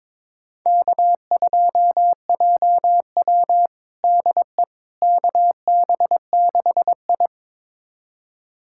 Morse code, heard as a keyed tone, about 22 wpm, 700 Hz. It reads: K2JW DE KB6I
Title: USA Callsign DE Callsign - 4 Characters - 22wpm